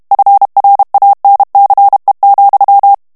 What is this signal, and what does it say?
Morse code, heard as a keyed tone, about 32 wpm, 800 Hz. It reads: FRANCE,
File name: morse mp3 / FRANCAIS TEXTE CLAIR 1 VITESSE 30 MOTS MINUTES